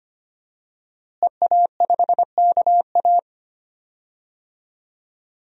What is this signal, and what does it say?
Morse code, heard as a keyed tone, about 25 wpm, 700 Hz. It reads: EA5KA
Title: DXCC-100 Call Signs - 5 Characters - Repeated 3x - 25wpm